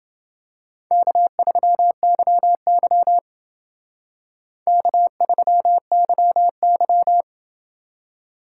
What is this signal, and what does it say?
Morse code, heard as a keyed tone, about 28 wpm, 700 Hz. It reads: K3YY K3YY